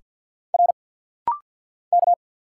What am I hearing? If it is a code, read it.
Morse code, heard as a keyed tone, about 50 wpm, 700 Hz. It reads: R K
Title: Single Letter-Number Alt - Rapid-Fire 50wpm